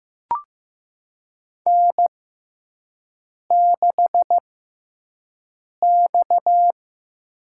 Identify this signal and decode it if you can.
Morse code, heard as a keyed tone, about 15 wpm, spaced as for 5 wpm, 700 Hz. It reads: N6X
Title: USA Call Signs - 5 Characters - 6x Character Spacing - 15wpm